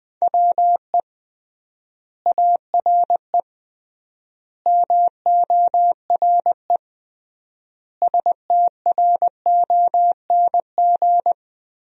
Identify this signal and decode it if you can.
Morse code, heard as a keyed tone, about 20 wpm, 700 Hz. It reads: WE ARE MORE STRONG
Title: 3x Word-Spacing - Sentences from Top 500 Words - 20wpm